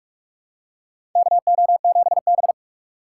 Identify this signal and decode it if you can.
Morse code, heard as a keyed tone, about 44 wpm, 690 Hz. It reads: KO7B
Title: USA Call Signs - 4 Characters - 2x Character Spacing - 45wpm